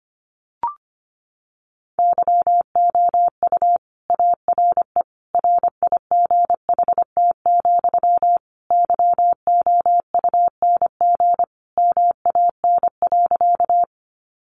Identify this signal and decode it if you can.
Morse code, heard as a keyed tone, about 25 wpm, 700 Hz. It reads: YOU ARE RIGHT, YOUNG MAN.
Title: Sentences from Top 400 Words 25wpm